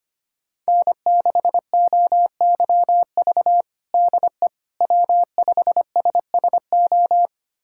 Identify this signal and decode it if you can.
Morse code, heard as a keyed tone, about 25 wpm, 700 Hz. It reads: N6OYV DE W5SSO
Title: USA Callsign DE Callsign - 5 Characters - 25wpm